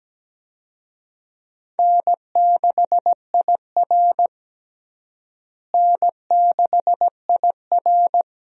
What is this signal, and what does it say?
Morse code, heard as a keyed tone, about 17 wpm, 700 Hz. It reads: N6IR N6IR
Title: USA Call Signs - 4 Characters - Repeated 2x - 17wpm